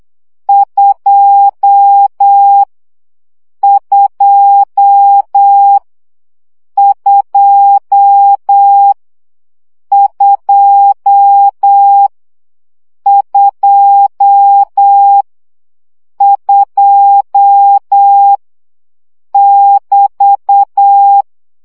Morse code, heard as a keyed tone, about 8 wpm, 800 Hz. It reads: 2 2 2 2 2 2 <BT>
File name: CW gyakorlas 2 7 baud